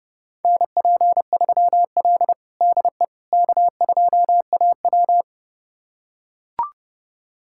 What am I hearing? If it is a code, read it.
Morse code, heard as a keyed tone, about 30 wpm, 700 Hz. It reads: NP3L DE K2AW
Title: USA Callsign DE Callsign - 4 Characters - 30wpm